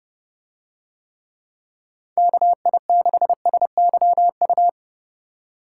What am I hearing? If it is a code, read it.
Morse code, heard as a keyed tone, about 29 wpm, 700 Hz. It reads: KI6SYU